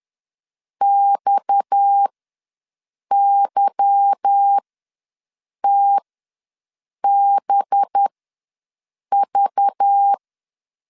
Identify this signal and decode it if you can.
Morse code, heard as a keyed tone, about 10 wpm, spaced as for 6 wpm, 800 Hz. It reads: XYTBV
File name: LEZIONI / 8 YJ45 / 8 YJ4 ws 10wpm 4020